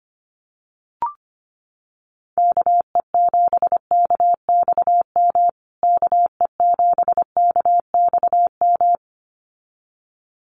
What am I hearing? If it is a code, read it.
Morse code, heard as a keyed tone, about 25 wpm, 700 Hz. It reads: KE7KXM KE7KXM
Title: USA Call Signs Repeated 2x 25wpm